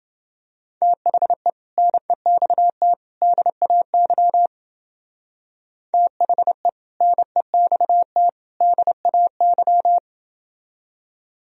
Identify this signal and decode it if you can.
Morse code, heard as a keyed tone, about 28 wpm, 700 Hz. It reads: THE NEXT DAY THE NEXT DAY